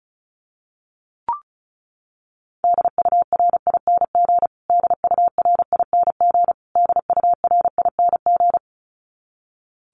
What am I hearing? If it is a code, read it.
Morse code, heard as a keyed tone, about 34 wpm, 700 Hz. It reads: DURING DURING DURING
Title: Top 500 Words Repeated 35wpm